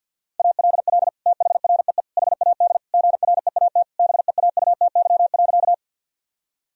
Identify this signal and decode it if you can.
Morse code, heard as a keyed tone, about 50 wpm, 700 Hz. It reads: ALL THREE HAD GREAT BEAUTY.
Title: Sentences from Top 1000 Words - Review Part 2 - 50wpm